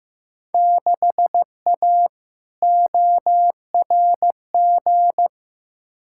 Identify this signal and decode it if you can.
Morse code, heard as a keyed tone, about 15 wpm, 700 Hz. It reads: 6A ORG
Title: Field Day Exchanges Encore - 15wpm